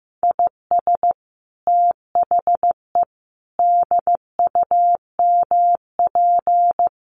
Morse code, heard as a keyed tone, about 15 wpm, 700 Hz. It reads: IS THE DUMP
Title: Sentences from Top 1300 Words 15wpm